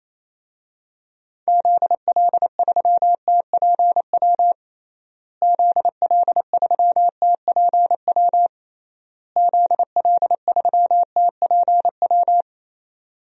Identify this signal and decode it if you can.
Morse code, heard as a keyed tone, about 27 wpm, 700 Hz. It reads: ZL3TPW ZL3TPW ZL3TPW
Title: DXCC-100 Call Signs - 6 Characters - Repeated 3x - 28wpm